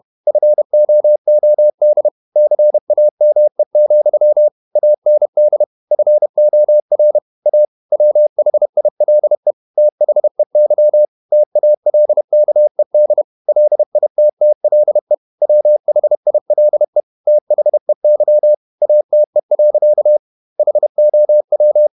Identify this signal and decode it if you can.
Morse code, heard as a keyed tone, about 31 wpm, 600 Hz. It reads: FOOD CAME, AND FOR A WHILE THEY TALKED LITTLE WHILE THEY ATE. HOW
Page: Never 31wpm | CW med Gnister